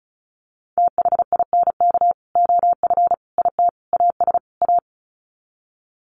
Morse code, heard as a keyed tone, about 34 wpm, 700 Hz. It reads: THINK OF IT AS A
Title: Sets of 5 Words Encore2 35wpm